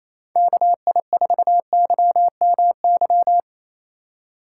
Morse code, heard as a keyed tone, about 27 wpm, 700 Hz. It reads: KI4YMY
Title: Speed Racing - USA 2x3 Call Signs 28wpm